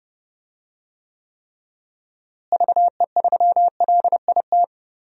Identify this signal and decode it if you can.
Morse code, heard as a keyed tone, about 31 wpm, 700 Hz. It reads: VE3LIT
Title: DXCC-100 Call Signs - 6 Characters - Repeated 3x - 30wpm